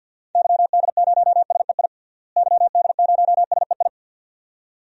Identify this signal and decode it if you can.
Morse code, heard as a keyed tone, about 53 wpm, 700 Hz. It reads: YD0SEI YD0SEI
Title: DXCC-100 Call Signs - 6 Characters - Repeated 2x - 50wpm